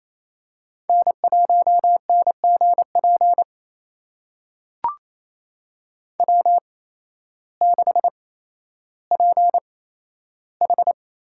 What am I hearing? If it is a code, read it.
Morse code, heard as a keyed tone, about 27 wpm, spaced as for 13 wpm, 700 Hz. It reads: N1NGP W6PH
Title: USA Call Signs - 8x Character Spacing - 28wpm